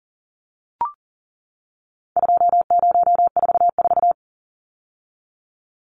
Morse code, heard as a keyed tone, about 40 wpm, 700 Hz. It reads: 2044
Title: Numbers - 4-Digit 40wpm